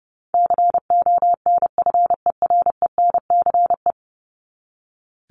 Morse code, heard as a keyed tone, about 30 wpm, 700 Hz. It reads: CONFERENCE
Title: Top 1751-2000 Words 30wpm